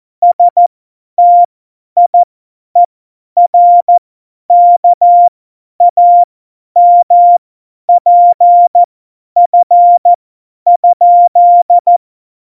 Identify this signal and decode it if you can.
Morse code, heard as a keyed tone, about 14 wpm, spaced as for 10 wpm, 700 Hz. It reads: STIERKAMPF?
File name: Morsekurs / Witze / Witz 17. September 2019, 50 F70